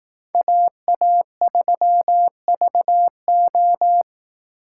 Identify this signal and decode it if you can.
Morse code, heard as a keyed tone, about 18 wpm, 700 Hz. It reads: AA3VO
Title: Speed Racing - USA 2x2 Call Signs 15wpm